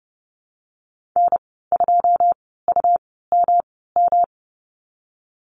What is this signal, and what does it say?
Morse code, heard as a keyed tone, about 32 wpm, spaced as for 17 wpm, 700 Hz. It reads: N2UMM
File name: Lesson 19g - Letter w - Call Signs 30wpm